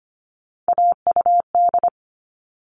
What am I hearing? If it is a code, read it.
Morse code, heard as a keyed tone, about 25 wpm, 700 Hz. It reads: AUD
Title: All Cut Numbers - 3-Digit 25wpm